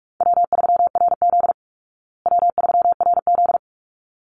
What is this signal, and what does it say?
Morse code, heard as a keyed tone, about 44 wpm, 700 Hz. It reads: W3RZ W3RZ
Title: Easy Call Signs Repeated 2x - 45wpm